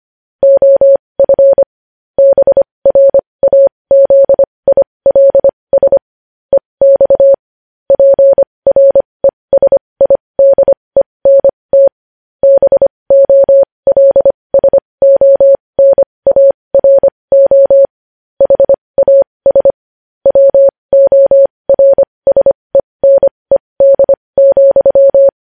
Morse code, heard as a keyed tone, about 25 wpm, spaced as for 20 wpm, 550 Hz. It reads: OF BRAZILS EX PRESIDENT BOLSONARO HAS WORSENED,
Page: News Headlines in Morse Code at 20 WPM